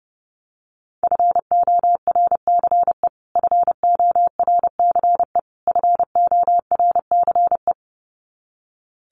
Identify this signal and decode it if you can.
Morse code, heard as a keyed tone, about 30 wpm, 700 Hz. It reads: FORCE FORCE FORCE
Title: Top 500 Words Repeated 30wpm